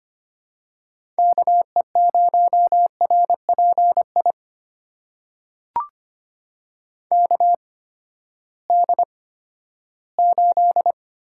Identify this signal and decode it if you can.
Morse code, heard as a keyed tone, about 25 wpm, spaced as for 14 wpm, 700 Hz. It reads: KE0RPI KD8
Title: USA Call Signs - 8x Character Spacing - 25wpm